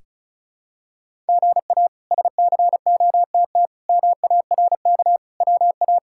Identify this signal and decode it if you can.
Morse code, heard as a keyed tone, about 34 wpm, 700 Hz. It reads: GA SCOTT MARK WA
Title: K1USN SST QSO Practice - 35wpm